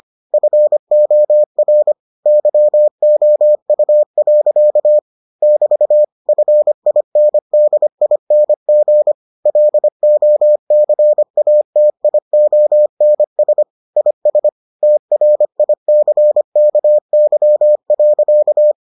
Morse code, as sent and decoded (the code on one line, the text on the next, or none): ..-. --- .-. / -.-- --- ..- .-.-.- / -...- / ..-. .. -. -.. .. -. --. / .-.. --- -.-. .- - .. --- -. ... / .. ... / - .-. .. -.-. -.- -.-- .-.-.-
FOR YOU. <BT> FINDING LOCATIONS IS TRICKY.